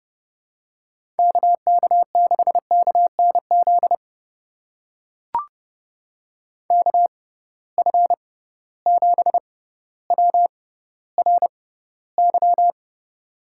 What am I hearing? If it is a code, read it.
Morse code, heard as a keyed tone, about 30 wpm, spaced as for 15 wpm, 700 Hz. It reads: KK6KNZ KF7WRY
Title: USA Call Signs - 6 Characters - 6x Character Spacing - 30wpm